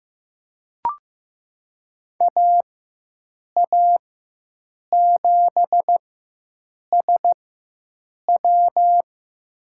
Morse code, heard as a keyed tone, about 15 wpm, spaced as for 7 wpm, 700 Hz. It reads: AA7SW
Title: USA Call Signs - 4x Character Spacing - 15wpm